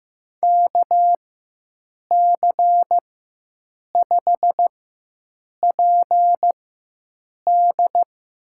Morse code, heard as a keyed tone, about 15 wpm, spaced as for 7 wpm, 700 Hz. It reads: KC5PD
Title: USA Call Signs - 4x Character Spacing - 15wpm